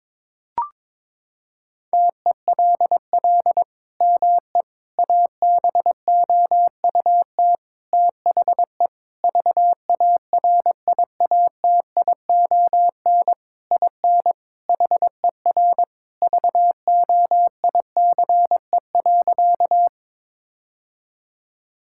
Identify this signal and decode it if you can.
Morse code, heard as a keyed tone, about 22 wpm, 700 Hz. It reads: TELL ME ABOUT THE VARIATION IN HER VOICE.
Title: Sentences from Top 1600 Words 22wpm